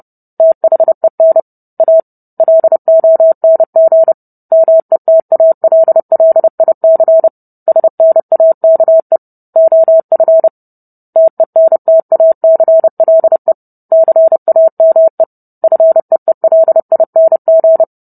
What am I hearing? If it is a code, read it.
Morse code, heard as a keyed tone, about 30 wpm, 650 Hz. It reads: THEN A LONG METALLIC SNAKE OF TENTACLE CAME FEELING